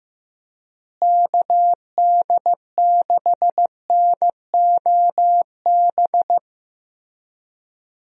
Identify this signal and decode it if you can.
Morse code, heard as a keyed tone, about 16 wpm, 700 Hz. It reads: KD6NOB
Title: USA Call Signs - 6 Characters - Repeated 3x - 15wpm